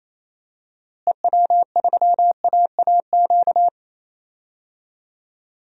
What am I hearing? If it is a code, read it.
Morse code, heard as a keyed tone, about 27 wpm, 700 Hz. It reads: EW3AAQ
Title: DXCC-100 Call Signs - 6 Characters - Repeated 2x - 28wpm